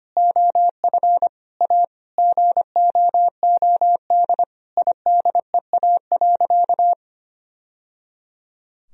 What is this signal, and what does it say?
Morse code, heard as a keyed tone, about 25 wpm, 700 Hz. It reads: OF A GOOD IDEA.
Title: Sentences from Top 1000 Words - Review Part 4 - 25wpm